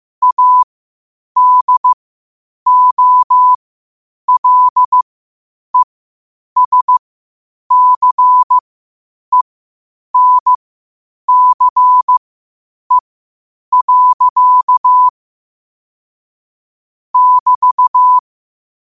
Morse code, heard as a keyed tone, about 15 wpm, spaced as for 8 wpm, 1000 Hz. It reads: ADOLESCENCE. <BT>
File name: qotdfiles / 8wpm